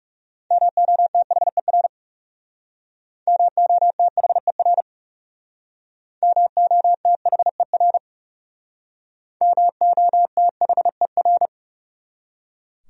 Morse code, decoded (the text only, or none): MOTHER MOTHER MOTHER MOTHER